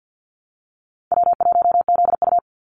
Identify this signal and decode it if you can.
Morse code, heard as a keyed tone, about 50 wpm, 700 Hz. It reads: W1ZU
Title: NA SOTA Activator Callsigns 50wpm